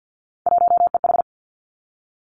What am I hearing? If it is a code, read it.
Morse code, heard as a keyed tone, about 50 wpm, 700 Hz. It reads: 1EH
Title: Three Challenging Letter-Number Combinations 50wpm